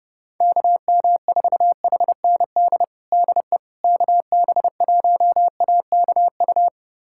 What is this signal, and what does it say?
Morse code, heard as a keyed tone, about 31 wpm, 700 Hz. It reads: KM4HND DE KB1AKU
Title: USA Callsign DE Callsign 30wpm